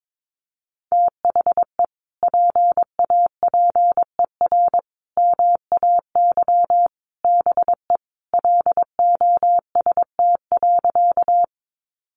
Morse code, heard as a keyed tone, about 22 wpm, 700 Hz. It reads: THE PAPER MAY BE LOST.
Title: Sentences from Top 1400 Words 22wpm